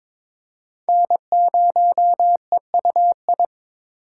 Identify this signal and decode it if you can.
Morse code, heard as a keyed tone, about 22 wpm, 700 Hz. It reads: N0EUI
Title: USA Call Signs - 5 Characters - 4x Character Spacing - 22wpm